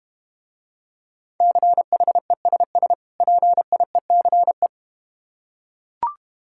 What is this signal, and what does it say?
Morse code, heard as a keyed tone, about 32 wpm, 700 Hz. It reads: CHESS PIECE